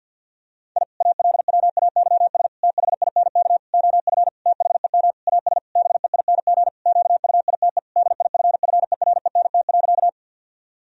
Morse code, heard as a keyed tone, about 50 wpm, 700 Hz. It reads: I ALWAYS THINK OF THEM AS BEING QUITE DIFFERENT.
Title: Sentences from Top 1000 Words - Review Part 3 - 50wpm